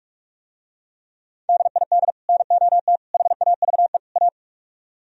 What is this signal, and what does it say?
Morse code, heard as a keyed tone, about 47 wpm, 700 Hz. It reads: DID NOT HAVE A